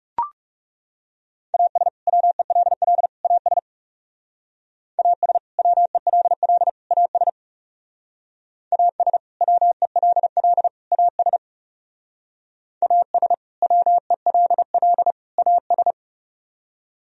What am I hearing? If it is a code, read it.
Morse code, heard as a keyed tone, about 37 wpm, 700 Hz. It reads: AS WELL AS AS WELL AS AS WELL AS AS WELL AS
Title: Speed-Racing - Sets of 3 Words 30wpm